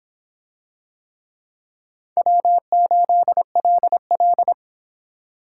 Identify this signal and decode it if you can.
Morse code, heard as a keyed tone, about 26 wpm, 700 Hz. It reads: W8LL